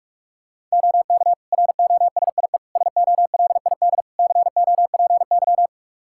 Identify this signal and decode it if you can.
Morse code, heard as a keyed tone, about 45 wpm, 700 Hz. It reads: OK ROSIE SOLID COPY
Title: Practice QSO - Exchange 3 - More Info - 45wpm